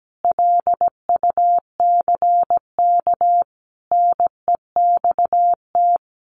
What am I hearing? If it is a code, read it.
Morse code, heard as a keyed tone, about 17 wpm, 700 Hz. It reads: LUCK NEXT
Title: Sentences from Top 1500 Words 17wpm